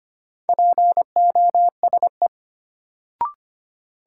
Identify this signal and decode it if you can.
Morse code, heard as a keyed tone, about 25 wpm, 700 Hz. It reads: POSE
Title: Sentences from Top 1000 Words - Review Part 2 - New Word Drill - 25wpm